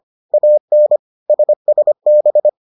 Morse code, heard as a keyed tone, about 25 wpm, 600 Hz. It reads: AN SSB